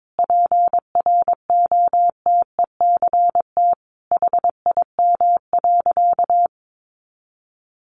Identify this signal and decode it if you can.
Morse code, heard as a keyed tone, about 22 wpm, 700 Hz. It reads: PROTECT HIM.
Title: Sentences from Top 900 Words 22wpm